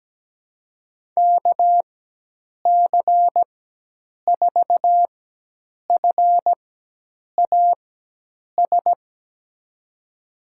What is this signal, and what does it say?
Morse code, heard as a keyed tone, about 17 wpm, spaced as for 8 wpm, 700 Hz. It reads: KC4FAS
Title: USA Call Signs - 4x Character Spacing - 17wpm